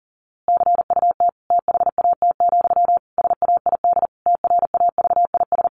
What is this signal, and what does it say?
Morse code, heard as a keyed tone, about 40 wpm, 700 Hz. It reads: CUT THAT, SAID TRAVIS
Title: A Sound Of Thunder 40wpm